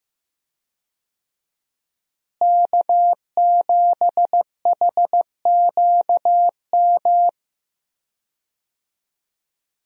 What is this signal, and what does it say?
Morse code, heard as a keyed tone, about 15 wpm, 700 Hz. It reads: K7HQM
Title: USA Call Signs - 5 Characters - Repeated 2x - 15wpm